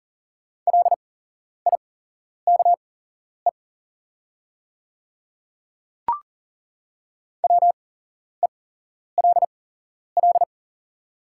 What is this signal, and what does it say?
Morse code, heard as a keyed tone, about 40 wpm, spaced as for 11 wpm, 690 Hz. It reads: LIKE WELL
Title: 8x Character Spacing - Top 300 Words - 40wpm